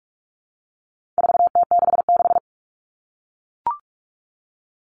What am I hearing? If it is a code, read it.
Morse code, heard as a keyed tone, about 44 wpm, 700 Hz. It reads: 4T66
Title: All Cut Numbers - 4-Digit 45wpm